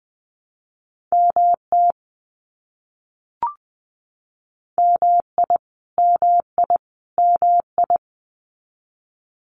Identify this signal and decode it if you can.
Morse code, heard as a keyed tone, about 20 wpm, 700 Hz. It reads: MT MI MI MI
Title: US States Repeated 3x 20wpm